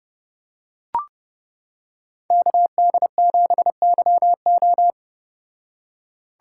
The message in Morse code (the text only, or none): KD7YO